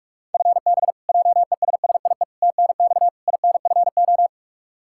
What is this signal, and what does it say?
Morse code, heard as a keyed tone, about 46 wpm, 700 Hz. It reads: FB JESSIE TNX INFO
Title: Practice QSO - Exchange 3 - More Info - 45wpm